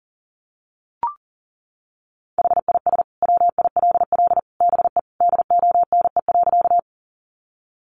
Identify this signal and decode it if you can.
Morse code, heard as a keyed tone, about 38 wpm, 700 Hz. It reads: HIS WILL BE DONE.
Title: Sentences from Top 500 Words 40wpm